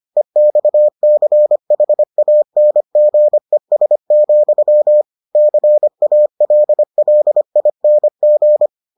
Morse code, heard as a keyed tone, about 25 wpm, 600 Hz. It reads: EXCHANGES, CALLING